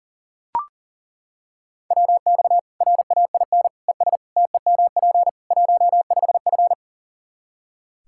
Morse code, heard as a keyed tone, about 40 wpm, 700 Hz. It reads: WX RAIN ES TEMP 15F
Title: Practice QSO - Exchange 2 - Info - 40wpm